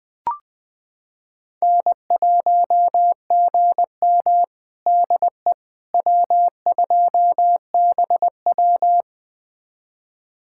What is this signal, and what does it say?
Morse code, heard as a keyed tone, about 21 wpm, 700 Hz. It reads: N1GM DE W2BW
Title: USA Callsign DE Callsign - 4 Characters - 20wpm